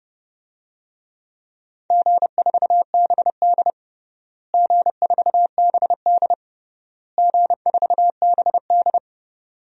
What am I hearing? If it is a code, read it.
Morse code, heard as a keyed tone, about 30 wpm, 700 Hz. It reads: G4BD G4BD G4BD
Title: DXCC-100 Call Signs - 4 Characters - Repeated 3x - 30wpm